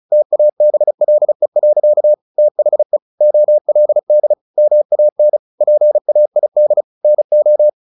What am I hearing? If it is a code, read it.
Morse code, heard as a keyed tone, about 34 wpm, 600 Hz. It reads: TABLE. THE OLD MAN PAID NO
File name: the old man and the sea 2 35wpm